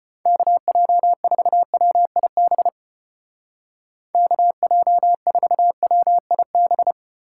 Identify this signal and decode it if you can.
Morse code, heard as a keyed tone, about 31 wpm, 700 Hz. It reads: KJ4WIB KJ4WIB
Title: Speed Racing - USA 2x3 Call Signs 25wpm